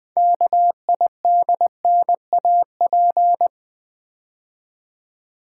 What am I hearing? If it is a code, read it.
Morse code, heard as a keyed tone, about 20 wpm, 700 Hz. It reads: KIDNAP
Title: 6 Letter Words - First 3 Letters Given - 20wpm